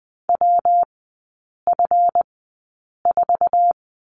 Morse code, heard as a keyed tone, about 20 wpm, spaced as for 8 wpm, 700 Hz. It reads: WF4
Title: 152 Lesson 25e - Letter p - Three Characters 20wpm